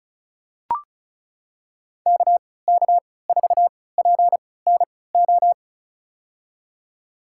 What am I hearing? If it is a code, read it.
Morse code, heard as a keyed tone, about 36 wpm, spaced as for 20 wpm, 700 Hz. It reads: KK4PNO
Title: USA Call Signs - 6 Characters - 3x Character Spacing - 35wpm